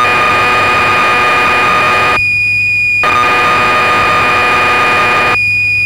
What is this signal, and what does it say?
Cassette data (Kansas City type).
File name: magazines / Acorn User / 1984-09 BBC Programs